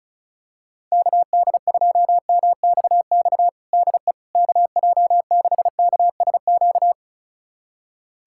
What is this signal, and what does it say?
Morse code, heard as a keyed tone, about 34 wpm, 700 Hz. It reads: KD2MXX DE KJ6KSQ
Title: USA Callsign DE Callsign 35wpm